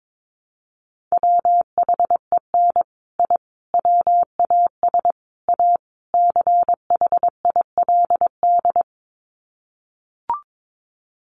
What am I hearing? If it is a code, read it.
Morse code, heard as a keyed tone, about 22 wpm, 700 Hz. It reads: WHEN I WAS A CHILD
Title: Sets of 5 Words Encore 22wpm